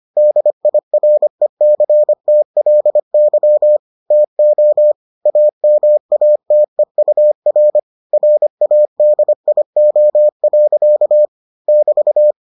Morse code, heard as a keyed tone, about 25 wpm, 600 Hz. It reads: DIRECTLY TO AMATEUR RADIO. <BT>